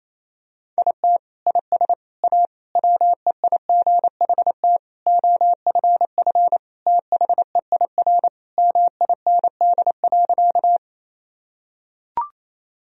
Morse code, heard as a keyed tone, about 28 wpm, 700 Hz. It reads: IT IS A WEIGHT OFF THEIR MIND.
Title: Sentences from Top 1000 Words - Review Part 3 - 28wpm